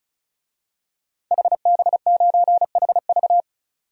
Morse code, heard as a keyed tone, about 34 wpm, 700 Hz. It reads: HB9HV
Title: DXCC-100 Call Signs - 5 Characters - 35wpm